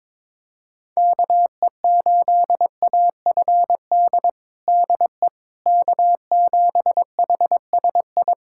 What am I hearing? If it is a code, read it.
Morse code, heard as a keyed tone, about 22 wpm, 700 Hz. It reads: KE8AFD DE K7HSI
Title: USA Callsign DE Callsign 22wpm